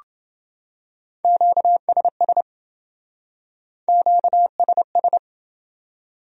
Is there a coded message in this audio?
Yes: Morse code from 1.2 to 5.2 s, heard as a keyed tone, about 28 wpm, 700 Hz. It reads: QSS QSS